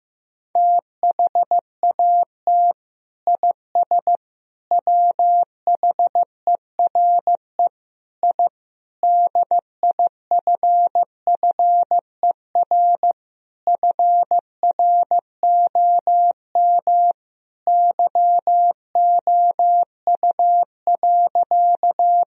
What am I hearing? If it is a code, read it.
Morse code, heard as a keyed tone, about 15 wpm, 700 Hz. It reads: THAT IS WHERE I DIFFER FROM YOU.
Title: Sentences from Top 1000 Words - Review Part 1 - 15wpm